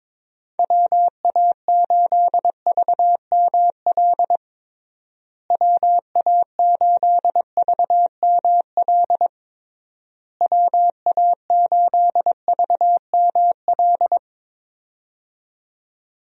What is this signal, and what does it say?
Morse code, heard as a keyed tone, about 22 wpm, 700 Hz. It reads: WA8VML WA8VML WA8VML
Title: USA Call Signs - 6 Characters - Repeated 3x - 22wpm